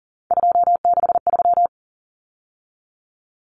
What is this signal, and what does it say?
Morse code, heard as a keyed tone, about 40 wpm, 700 Hz. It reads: 263
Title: Numbers - 3-Digit 40wpm